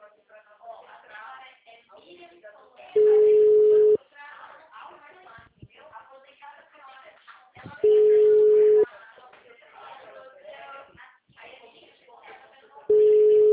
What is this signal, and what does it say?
Ringback tone (Europe, 425 Hz), with speech.
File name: out- -2015- - - .62931